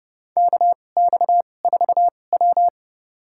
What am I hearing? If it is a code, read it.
Morse code, heard as a keyed tone, about 31 wpm, spaced as for 22 wpm, 700 Hz. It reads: KX4W
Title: USA Call Signs - 4 Characters - 2x Character Spacing - 30wpm